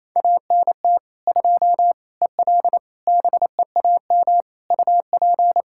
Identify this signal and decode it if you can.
Morse code, heard as a keyed tone, about 28 wpm, 700 Hz. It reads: ANT 2 EL BEAM UP
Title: Practice QSO - Exchange 2 - Info - 28wpm